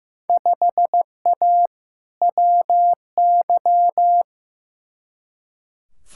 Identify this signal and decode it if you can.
Morse code, heard as a keyed tone, about 15 wpm, 700 Hz. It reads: 5A WY
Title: Field Day Exchanges Encore - 15wpm